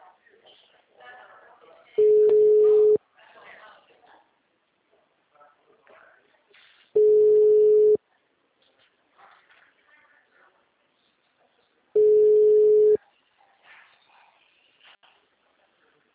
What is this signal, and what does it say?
Ringback tone (Europe, 425 Hz), with speech.